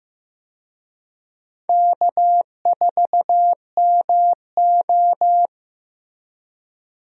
Morse code, heard as a keyed tone, about 15 wpm, 700 Hz. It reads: K4MO
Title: USA Call Signs - 4 Characters - Repeated 3x - 15wpm